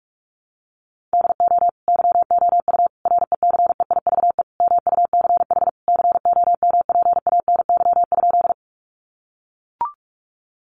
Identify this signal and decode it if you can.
Morse code, heard as a keyed tone, about 44 wpm, 700 Hz. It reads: DO YOU RECEIVE MUCH COMPANY?
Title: Sentences from Top 1000 Words 45wpm